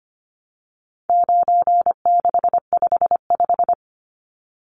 Morse code, heard as a keyed tone, about 25 wpm, 700 Hz. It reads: 9655
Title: Numbers - 4-Digit 25wpm